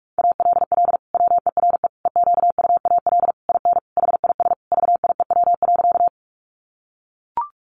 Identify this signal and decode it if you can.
Morse code, heard as a keyed tone, about 44 wpm, 700 Hz. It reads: ALL WERE EQUAL IN HIS VIEW.
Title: Sentences from Top 1400 Words 45wpm